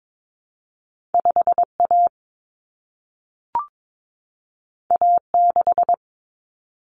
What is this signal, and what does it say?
Morse code, heard as a keyed tone, about 22 wpm, 700 Hz. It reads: 5A A6
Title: Common Cut Numbers - 2-Digit 22wpm